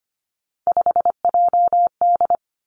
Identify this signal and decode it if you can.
Morse code, heard as a keyed tone, about 25 wpm, 700 Hz. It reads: 5JD
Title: Three Challenging Letter-Number Combinations 25wpm